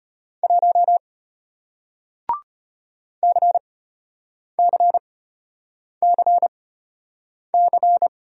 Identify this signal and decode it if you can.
Morse code, heard as a keyed tone, about 33 wpm, spaced as for 8 wpm, 700 Hz. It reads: 1 CCCC